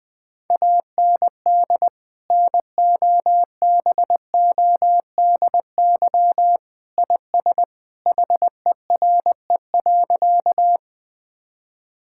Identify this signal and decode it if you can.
Morse code, heard as a keyed tone, about 21 wpm, 700 Hz. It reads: AND NOBODY IS HERE.
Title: Sentences from Top 1600 Words 20wpm